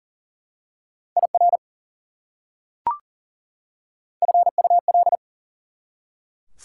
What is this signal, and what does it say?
Morse code, heard as a keyed tone, about 40 wpm, 700 Hz. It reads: IR FUL
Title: Prefixes and Suffixes 40wpm